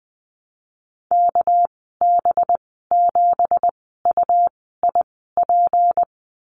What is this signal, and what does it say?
Morse code, heard as a keyed tone, about 20 wpm, spaced as for 14 wpm, 700 Hz. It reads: KB7UIP
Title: 207 Lesson 33g - Letter j - Call Signs 20wpm